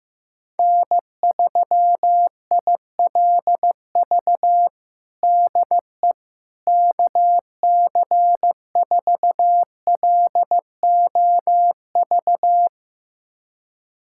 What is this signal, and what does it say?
Morse code, heard as a keyed tone, about 15 wpm, 700 Hz. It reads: N3ILV DE KC4LOV
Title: USA Callsign DE Callsign 15wpm